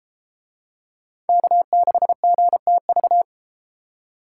Morse code, heard as a keyed tone, about 33 wpm, 700 Hz. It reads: K6GTV